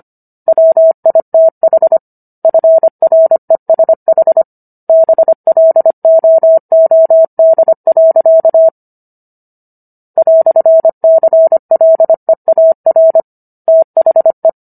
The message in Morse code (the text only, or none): WITH FRESH BLOOD. "CLEAR THE